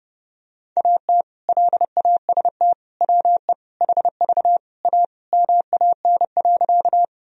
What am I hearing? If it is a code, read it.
Morse code, heard as a keyed tone, about 30 wpm, 700 Hz. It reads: AT LAST WE HV A MAN.
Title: Sentences from Top 300 Words and Abbreviations 30wpm